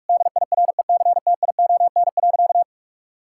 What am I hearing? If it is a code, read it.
Morse code, heard as a keyed tone, about 46 wpm, 700 Hz. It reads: DIRECTION.
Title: Sentences from Top 1000 Words - Review Part 5 - 45wpm